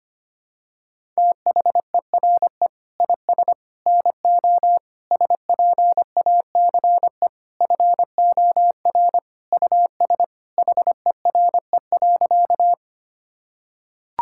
Morse code, heard as a keyed tone, about 25 wpm, 700 Hz. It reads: THERE IS NO SPACE FOR US HERE.
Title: Sentences from Top 1000 Words - Review Part 5 - 25wpm